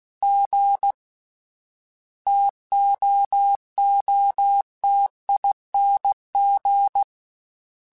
Morse code, heard as a keyed tone, about 16 wpm, 800 Hz. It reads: G TOOTING
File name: Difficult Combos / Difficult Combos-mp3s / T AOEI @15